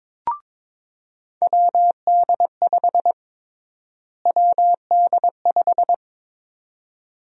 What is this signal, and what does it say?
Morse code, heard as a keyed tone, about 23 wpm, 700 Hz. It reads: WD5 WD5
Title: USA Call Sign Prefixes - Repeated 2x - 22wpm